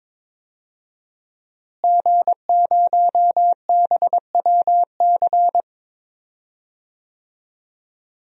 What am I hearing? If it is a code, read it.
Morse code, heard as a keyed tone, about 22 wpm, 700 Hz. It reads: G0BWC
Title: DXCC-100 Call Signs - 5 Characters - Repeated 3x - 22wpm